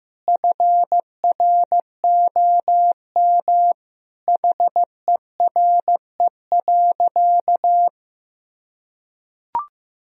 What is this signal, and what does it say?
Morse code, heard as a keyed tone, about 14 wpm, 700 Hz. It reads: FROM HERE.
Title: Sentences from Top 1000 Words - Review Part 4 - 15wpm